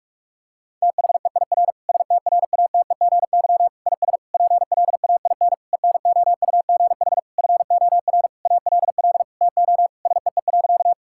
Morse code, heard as a keyed tone, about 41 wpm, 700 Hz. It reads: THEIR STRATEGY IS PLAIN ENOUGH FOR ALL TO SEE.
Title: Sentences from Top 1000 Words - Review Part 5 - 45wpm